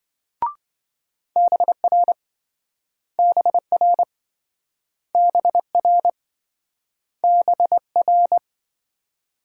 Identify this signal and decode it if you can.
Morse code, heard as a keyed tone, about 28 wpm, 700 Hz. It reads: BR BR BR BR